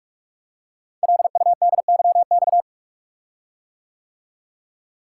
Morse code, heard as a keyed tone, about 48 wpm, 690 Hz. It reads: LUDYX
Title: Code Groups - 5 Letters - 45wpm